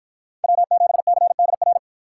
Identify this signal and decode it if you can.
Morse code, heard as a keyed tone, about 53 wpm, 700 Hz. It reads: W7CDR